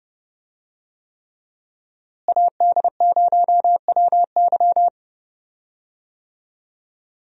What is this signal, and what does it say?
Morse code, heard as a keyed tone, about 30 wpm, 700 Hz. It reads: AD0WY
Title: USA Call Signs - 5 Characters - Repeated 3x - 30wpm